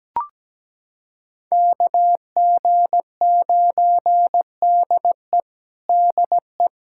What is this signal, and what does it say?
Morse code, heard as a keyed tone, about 17 wpm, 700 Hz. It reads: KG9DE DE
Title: USA Callsign DE Callsign - 5 Characters - 17wpm